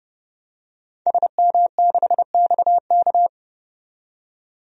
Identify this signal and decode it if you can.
Morse code, heard as a keyed tone, about 32 wpm, 700 Hz. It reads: SM6XK
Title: DXCC-100 Call Signs - 5 Characters - 30wpm